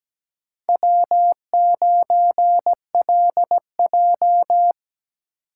Morse code, heard as a keyed tone, about 20 wpm, 700 Hz. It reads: W9LJ
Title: Speed Racing - USA 1x2 Call Signs 17wpm